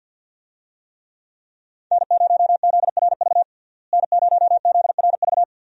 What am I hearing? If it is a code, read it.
Morse code, heard as a keyed tone, about 46 wpm, 700 Hz. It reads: N0ZRV N0ZRV
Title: USA Call Signs - 5 Characters - Repeated 3x - 50wpm